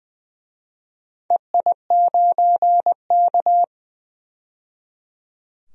Morse code, heard as a keyed tone, about 20 wpm, 700 Hz. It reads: EI9K
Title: DXCC-100 Call Signs - 4 Characters - 20wpm